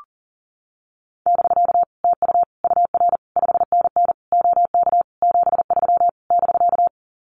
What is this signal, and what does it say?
Morse code, heard as a keyed tone, about 40 wpm, 700 Hz. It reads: <BK> TU UR 5NN OK 73 <BK>
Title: Common QSO Phrases Encore 40wpm